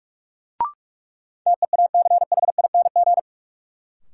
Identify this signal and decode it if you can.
Morse code, heard as a keyed tone, about 46 wpm, 700 Hz. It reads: TEACHING
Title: Sentences from Top 1100 Words - New Word Drill - 45wpm